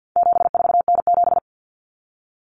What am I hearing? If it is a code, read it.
Morse code, heard as a keyed tone, about 50 wpm, 700 Hz. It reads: 74N7
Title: Common Cut Numbers - 4-Digit 50wpm